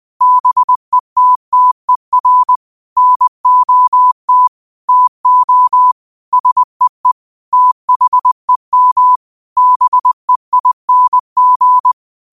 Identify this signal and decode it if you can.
Morse code, heard as a keyed tone, about 20 wpm, 1000 Hz. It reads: BETTER NOT TO SEE THEM BEING
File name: qotdfiles / 20wpm